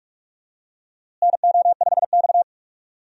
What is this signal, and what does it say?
Morse code, heard as a keyed tone, about 46 wpm, 700 Hz. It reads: NO5X
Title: USA Call Signs - 4 Characters - Repeated 2x - 45wpm